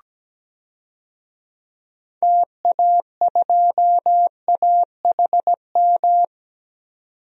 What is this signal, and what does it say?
Morse code, heard as a keyed tone, about 17 wpm, 700 Hz. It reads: TA2AHM
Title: DXCC-100 Call Signs - 6 Characters - 17wpm